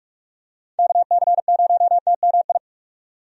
Morse code, heard as a keyed tone, about 42 wpm, 700 Hz. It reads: KC0TMI
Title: USA Call Signs - 6 Characters - 45wpm